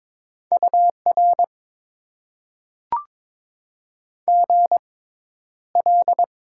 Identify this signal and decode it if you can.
Morse code, heard as a keyed tone, about 22 wpm, spaced as for 11 wpm, 700 Hz. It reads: UR GL
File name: Top 90 QSO Elements - 6x Character Spacing - 22wpm